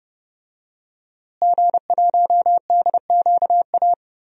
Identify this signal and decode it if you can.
Morse code, heard as a keyed tone, about 31 wpm, 700 Hz. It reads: G1DQA
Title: DXCC-100 Call Signs - 5 Characters - Repeated 2x - 30wpm